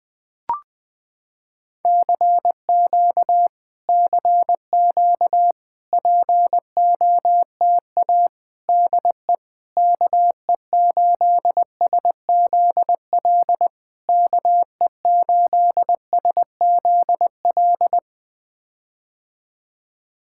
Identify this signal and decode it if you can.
Morse code, heard as a keyed tone, about 20 wpm, 700 Hz. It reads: CQ CQ POTA DE KE8SZL KE8SZL
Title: POTA QSO Practice - 20wpm